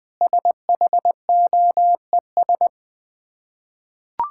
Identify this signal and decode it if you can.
Morse code, heard as a keyed tone, about 20 wpm, 700 Hz. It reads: SHOES
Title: Difficult Words with Hints - 20wpm